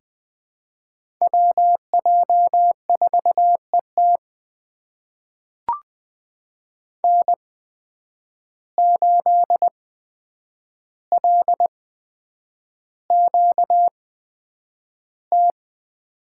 Morse code, heard as a keyed tone, about 20 wpm, spaced as for 9 wpm, 700 Hz. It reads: WJ4ET N8LQT
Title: USA Call Signs - 5 Characters - 8x Character Spacing - 20wpm